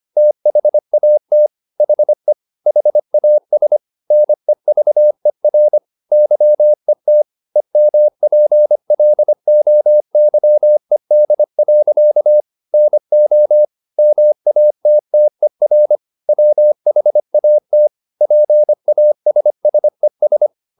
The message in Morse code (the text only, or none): THAT HE HAS NEVER YET EMPLOYED. NO MATTER WHAT PASSES